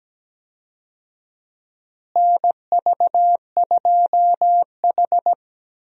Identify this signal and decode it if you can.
Morse code, heard as a keyed tone, about 17 wpm, 700 Hz. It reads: NV2H
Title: USA Call Signs - 4 Characters - Repeated 2x - 17wpm